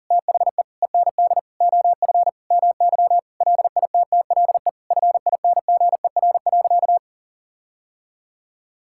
Morse code, heard as a keyed tone, about 40 wpm, 700 Hz. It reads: THE END OF MY LITTLE FINGER.
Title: Sentences from Top 1600 Words 40wpm